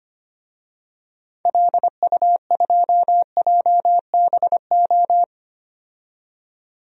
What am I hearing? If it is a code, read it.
Morse code, heard as a keyed tone, about 25 wpm, 700 Hz. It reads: LU2JBO
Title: DXCC-100 Call Signs - 6 Characters - 25wpm